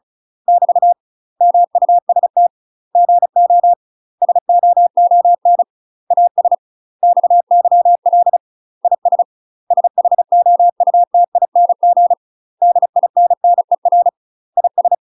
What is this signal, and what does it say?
Morse code, heard as a keyed tone, about 36 wpm, 700 Hz. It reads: <BT> MUST GO SOON AS XYL IS SHOUTING DINNER IS
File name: UK MASTER QSOs TRAINING / 35WPM / UK MASTER QSOs 0002